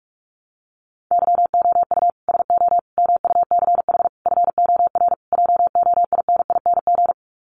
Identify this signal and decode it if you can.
Morse code, heard as a keyed tone, about 44 wpm, 700 Hz. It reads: YOU SO MUCH FOR JOINING
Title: Sets of 5 Words Encore2 45wpm